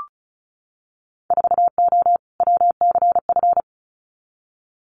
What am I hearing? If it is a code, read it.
Morse code, heard as a keyed tone, about 34 wpm, 700 Hz. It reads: 4O WCF
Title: Winter Field Day Exchanges Encore 35wpm